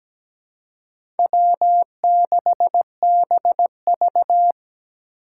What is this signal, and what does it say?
Morse code, heard as a keyed tone, about 17 wpm, 700 Hz. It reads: W6BV
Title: Four Challenging Letter-Number Combinations 17wpm